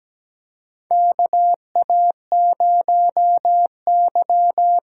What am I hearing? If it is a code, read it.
Morse code, heard as a keyed tone, about 17 wpm, 700 Hz. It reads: KA0Y
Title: USA Call Signs - 6 Characters - 8x Character Spacing - 17wpm